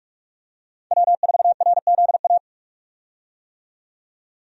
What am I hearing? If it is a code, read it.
Morse code, heard as a keyed tone, about 46 wpm, 700 Hz. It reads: W4RZA
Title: USA Call Signs - 5 Characters - 45wpm